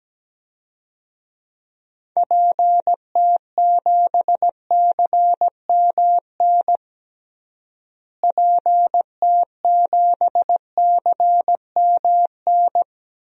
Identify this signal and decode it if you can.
Morse code, heard as a keyed tone, about 17 wpm, 700 Hz. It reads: PT7CMN PT7CMN
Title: DXCC-100 Call Signs - 6 Characters - Repeated 2x - 17wpm